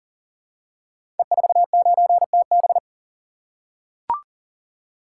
Morse code, heard as a keyed tone, about 38 wpm, 700 Hz. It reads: E49TB
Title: Code Groups - 5 Letter:Numbers - 40wpm